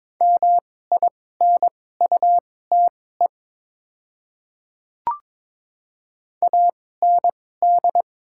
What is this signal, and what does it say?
Morse code, heard as a keyed tone, about 22 wpm, spaced as for 16 wpm, 700 Hz. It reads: MINUTE AND
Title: 2x Character Spacing - Top 500 Words - 22wpm